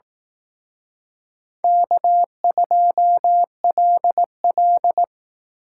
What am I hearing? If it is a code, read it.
Morse code, heard as a keyed tone, about 18 wpm, 700 Hz. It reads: K2LL
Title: Speed Racing - USA 1x2 Call Signs 15wpm